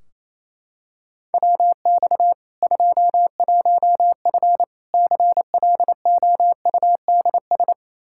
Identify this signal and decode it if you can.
Morse code, heard as a keyed tone, about 28 wpm, 700 Hz. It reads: WX 21F CLOUDS
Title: K1USN SST QSO Practice 28wpm